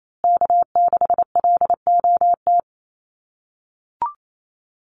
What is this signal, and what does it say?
Morse code, heard as a keyed tone, about 28 wpm, 700 Hz. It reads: K6LOT
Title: USA Call Signs Repeated 2x 28wpm